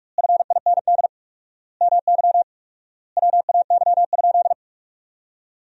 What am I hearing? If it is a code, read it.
Morse code, heard as a keyed tone, about 46 wpm, 700 Hz. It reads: FIND MY WAY?
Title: 4x Word-Spacing - Sentences from Top 200 Words - 45wpm